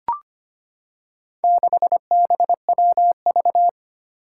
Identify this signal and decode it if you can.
Morse code, heard as a keyed tone, about 25 wpm, 700 Hz. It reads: 6BWV
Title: Four Challenging Letter-Number Combinations 25wpm